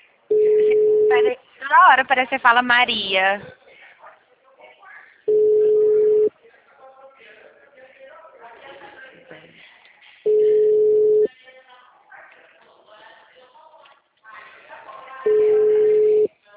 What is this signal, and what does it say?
Ringback tone (Europe, 425 Hz), with speech.
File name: out- -2034- - - .809